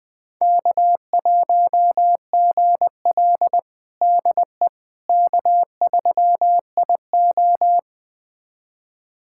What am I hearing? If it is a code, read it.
Morse code, heard as a keyed tone, about 20 wpm, 700 Hz. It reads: K1GL DE K3IO
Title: USA Callsign DE Callsign - 4 Characters - 20wpm